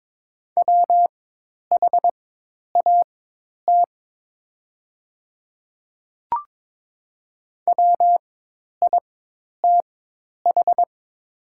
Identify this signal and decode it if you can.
Morse code, heard as a keyed tone, about 22 wpm, spaced as for 10 wpm, 700 Hz. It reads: WHAT WITH
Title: 4x Character Spacing - Top 100 Words - 22wpm